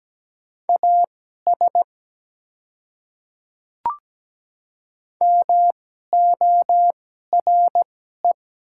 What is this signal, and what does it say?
Morse code, heard as a keyed tone, about 17 wpm, spaced as for 12 wpm, 700 Hz. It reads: AS MORE
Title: 2x Character Spacing - Top 100 Words - 17wpm